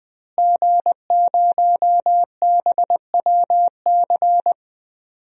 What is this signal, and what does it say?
Morse code, heard as a keyed tone, about 21 wpm, 700 Hz. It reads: G0BWC
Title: DXCC-100 Call Signs - 5 Characters - Repeated 3x - 20wpm